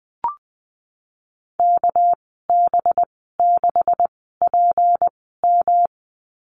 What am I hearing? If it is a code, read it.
Morse code, heard as a keyed tone, about 19 wpm, 700 Hz. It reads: K B 6 P M
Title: 200 Lesson 32g - Letter k - Call Signs 20wpm